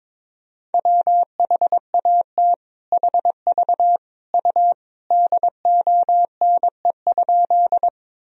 Morse code, heard as a keyed tone, about 22 wpm, 700 Hz. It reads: WHAT HV U DONE?
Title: Sentences from Top 500 Words and Abbreviations 22wpm